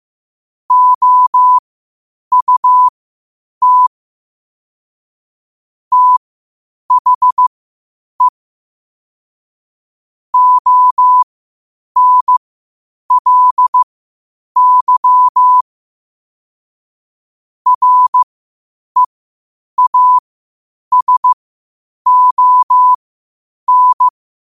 Morse code, heard as a keyed tone, about 15 wpm, spaced as for 8 wpm, 1000 Hz. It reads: OUT THE ONLY REASON
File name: qotdfiles / 8wpm